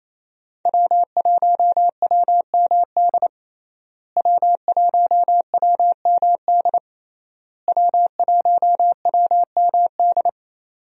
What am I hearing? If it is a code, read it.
Morse code, heard as a keyed tone, about 28 wpm, 700 Hz. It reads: W1WMD W1WMD W1WMD
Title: USA Call Signs - 5 Characters - Repeated 3x - 28wpm